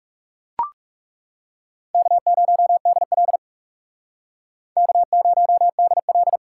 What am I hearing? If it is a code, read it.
Morse code, heard as a keyed tone, about 39 wpm, 700 Hz. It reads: K0DL K0DL